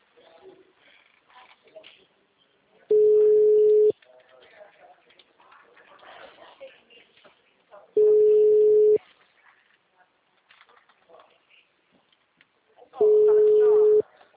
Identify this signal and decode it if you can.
Ringback tone (Europe, 425 Hz), with speech.